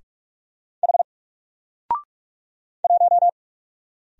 Morse code, heard as a keyed tone, about 44 wpm, 700 Hz. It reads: H 1
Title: Single Letter-Number Alt 45wpm